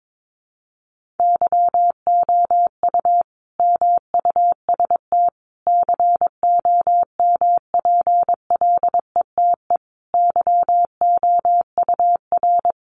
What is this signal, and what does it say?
Morse code, heard as a keyed tone, about 22 wpm, 700 Hz. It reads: YOU MUST COMPLETE YOUR
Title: Sentences from Top 1400 Words 22wpm